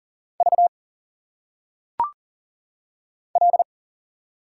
Morse code, heard as a keyed tone, about 40 wpm, 700 Hz. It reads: V L
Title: Single Letters - Phonetic Pronunciation 40wpm